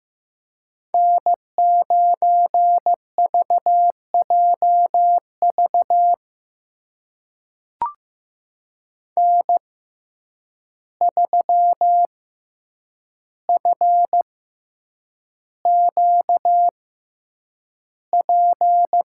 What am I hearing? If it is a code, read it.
Morse code, heard as a keyed tone, about 15 wpm, spaced as for 8 wpm, 700 Hz. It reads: N9VJV N3FQP
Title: USA Call Signs - 5 Characters - 6x Character Spacing - 15wpm